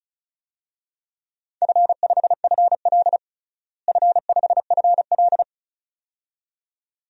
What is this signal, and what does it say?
Morse code, heard as a keyed tone, about 36 wpm, 700 Hz. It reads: F5FL F5FL
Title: DXCC-100 Call Signs - 4 Characters - Repeated 2x - 35wpm